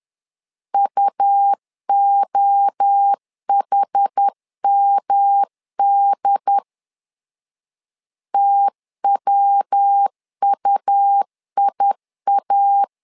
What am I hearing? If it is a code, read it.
Morse code, heard as a keyed tone, about 10 wpm, 800 Hz. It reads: UOHMD TWUIA
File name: LEZIONI / 4 Q1 10wpm 2448